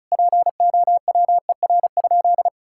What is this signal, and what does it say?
Morse code, heard as a keyed tone, about 36 wpm, 700 Hz. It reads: POWER?
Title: Sentences from Top 1000 Words - Review Part 5 - 35wpm